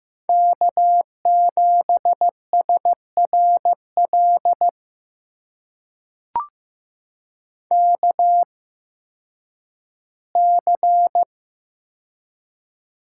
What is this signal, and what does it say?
Morse code, heard as a keyed tone, about 15 wpm, spaced as for 10 wpm, 700 Hz. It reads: K7SRL KC
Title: USA Call Signs - 8x Character Spacing - 15wpm